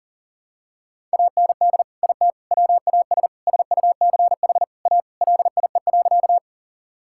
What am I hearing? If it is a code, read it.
Morse code, heard as a keyed tone, about 37 wpm, 700 Hz. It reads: AND IT WAS SUCH A LIE.
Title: Sentences from Top 1600 Words 40wpm